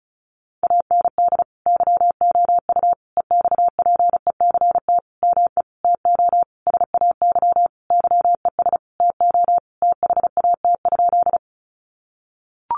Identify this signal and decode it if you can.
Morse code, heard as a keyed tone, about 34 wpm, 700 Hz. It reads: AND YOU EXPECT ME TO SAY YES TO THAT?
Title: Sentences from Top 1100 Words 35wpm